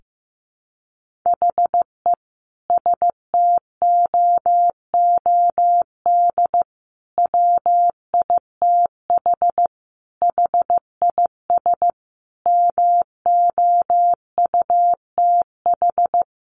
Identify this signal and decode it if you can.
Morse code, heard as a keyed tone, about 15 wpm, 700 Hz. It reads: HE STOOD WITH HIS MOUTH
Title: Sentences from Top 1500 Words 15wpm